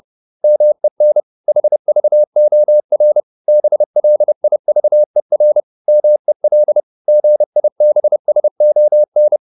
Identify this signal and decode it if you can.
Morse code, heard as a keyed tone, about 30 wpm, 600 Hz. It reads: MEN HVOR BLIVER MEL GIBSON
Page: Ferie 30 wpm